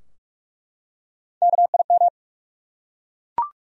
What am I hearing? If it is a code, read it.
Morse code, heard as a keyed tone, about 43 wpm, 700 Hz. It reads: KIM
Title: USA Common Names - Max 4 Letters - 45wpm